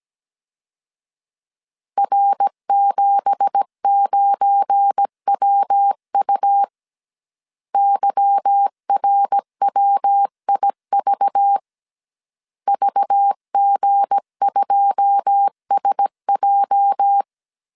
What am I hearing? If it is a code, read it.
Morse code, heard as a keyed tone, about 17 wpm, 800 Hz. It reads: R79WU YRWIV VG2SJ
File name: CW / DET / 1 9 1500C S C 16wpm 2909